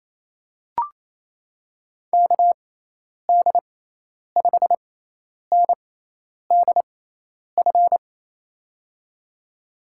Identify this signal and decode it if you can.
Morse code, heard as a keyed tone, about 28 wpm, spaced as for 10 wpm, 700 Hz. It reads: KD5NDF
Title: USA Call Signs - 6 Characters - 6x Character Spacing - 28wpm